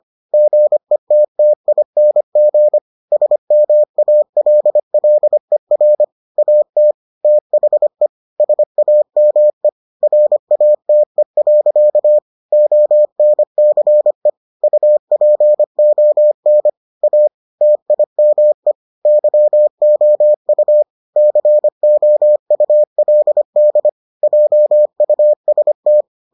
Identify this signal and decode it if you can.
Morse code, heard as a keyed tone, about 25 wpm, 600 Hz. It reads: GETTING SMALLER AT THE SAME RATE. ONCE UPON A TIME YOU COULD JUST